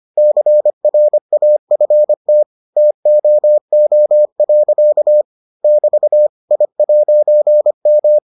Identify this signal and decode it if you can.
Morse code, heard as a keyed tone, about 25 wpm, 600 Hz. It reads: CRAFT TOO. <BT> I'M